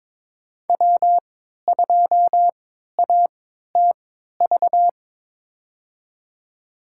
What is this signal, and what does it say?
Morse code, heard as a keyed tone, about 22 wpm, spaced as for 13 wpm, 700 Hz. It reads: W2ATV
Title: USA Call Signs - 5 Characters - 3x Character Spacing - 22wpm